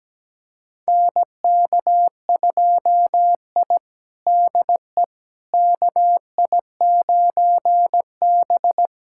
Morse code, heard as a keyed tone, about 17 wpm, 700 Hz. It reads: NK2I DE KI9B
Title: USA Callsign DE Callsign - 4 Characters - 17wpm